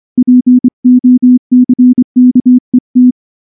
Morse code, heard as a keyed tone, about 24 wpm, 250 Hz. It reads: POCKET
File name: morse / morse-training-25WPM